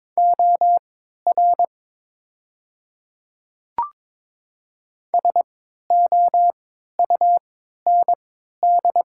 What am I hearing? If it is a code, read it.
Morse code, heard as a keyed tone, about 22 wpm, spaced as for 12 wpm, 700 Hz. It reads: OR SOUND
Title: 3x Character Spacing - Top 200 Words - 22wpm